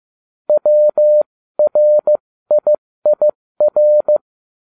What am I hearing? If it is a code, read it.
Morse code, heard as a keyed tone, about 15 wpm, spaced as for 13 wpm, 600 Hz. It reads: WRIIR